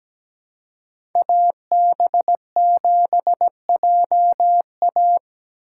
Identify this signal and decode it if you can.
Morse code, heard as a keyed tone, about 17 wpm, 700 Hz. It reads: AB7JA
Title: USA Call Signs - 5 Characters - 2x Character Spacing - 17wpm